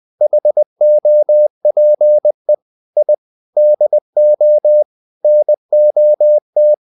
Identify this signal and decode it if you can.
Morse code, heard as a keyed tone, about 20 wpm, 600 Hz. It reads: HOPE I DO NOT
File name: the old man and the sea part 2 20wpm